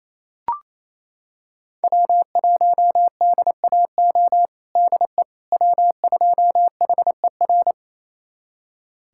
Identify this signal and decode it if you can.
Morse code, heard as a keyed tone, about 27 wpm, 700 Hz. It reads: W1DAO DE W2HER
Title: USA Callsign DE Callsign - 5 Characters - 28wpm